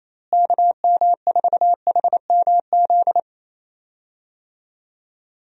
Morse code, heard as a keyed tone, about 28 wpm, 700 Hz. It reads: KM4HMZ
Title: USA Call Signs - 6 Characters - Repeated 3x - 28wpm